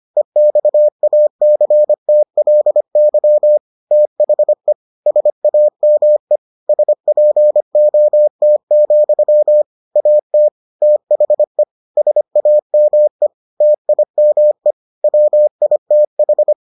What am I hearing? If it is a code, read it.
Morse code, heard as a keyed tone, about 25 wpm, 600 Hz. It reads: EXACTLY THE SAME SPOT, AT THE SAME TIME WITH